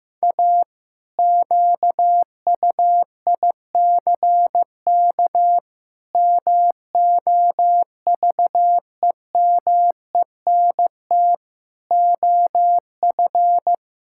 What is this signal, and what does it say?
Morse code, heard as a keyed tone, about 16 wpm, 700 Hz. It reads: A QUICK MOVEMENT OF
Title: Pangram Challenge v2 - 15wpm